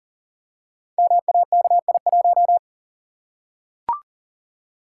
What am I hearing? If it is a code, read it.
Morse code, heard as a keyed tone, about 39 wpm, 700 Hz. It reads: MAKI1
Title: Code Groups - 5 Letter:Numbers - 40wpm